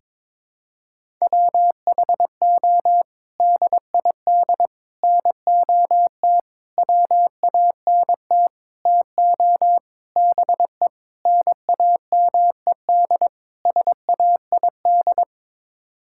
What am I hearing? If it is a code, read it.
Morse code, heard as a keyed tone, about 22 wpm, 700 Hz. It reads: WHO DID NOT WANT TO BE NAMED SAID
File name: practice / 22wpm / Sets of 8 Words - 22wpm